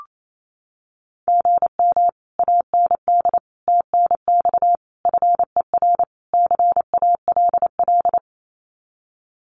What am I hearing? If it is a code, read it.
Morse code, heard as a keyed tone, about 28 wpm, 700 Hz. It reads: GM AND TNX FER CALL
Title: Common QSO Phrases Encore 28wpm